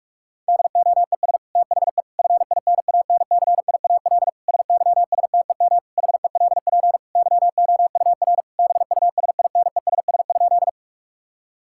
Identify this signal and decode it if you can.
Morse code, heard as a keyed tone, about 43 wpm, 700 Hz. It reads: DOES THE FINANCIAL SYSTEM HELP YOUR BUSINESS?
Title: Sentences from Top 1000 Words - Review Part 5 - 45wpm